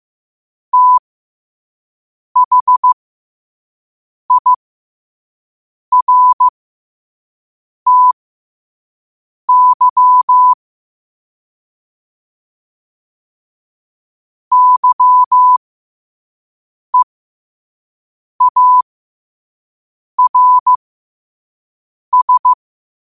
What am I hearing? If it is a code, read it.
Morse code, heard as a keyed tone, about 15 wpm, spaced as for 5 wpm, 1000 Hz. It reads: THIRTY YEARS
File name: qotdfiles / 5wpm